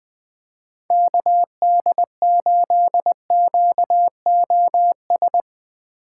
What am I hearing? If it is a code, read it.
Morse code, heard as a keyed tone, about 20 wpm, 700 Hz. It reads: KD8QOS
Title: USA Call Signs - 6 Characters - 8x Character Spacing - 20wpm